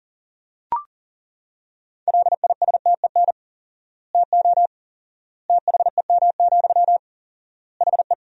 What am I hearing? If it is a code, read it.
Morse code, heard as a keyed tone, about 40 wpm, 700 Hz. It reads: LISTEN TO THEM, HE
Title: 4x Word-Spacing - Sentences from Top 400 Words - 40wpm